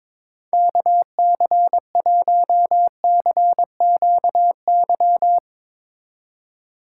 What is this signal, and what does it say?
Morse code, heard as a keyed tone, about 22 wpm, 700 Hz. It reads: KC1CQY
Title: USA Call Signs - 6 Characters - 6x Character Spacing - 22wpm